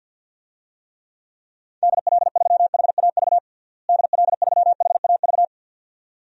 Morse code, heard as a keyed tone, about 50 wpm, 700 Hz. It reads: DL3HAV DL3HAV
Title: DXCC-100 Call Signs - 6 Characters - Repeated 2x - 50wpm